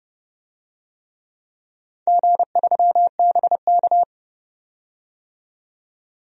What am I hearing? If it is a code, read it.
Morse code, heard as a keyed tone, about 30 wpm, 700 Hz. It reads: G3BK
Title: DXCC-100 Call Signs - 4 Characters - 30wpm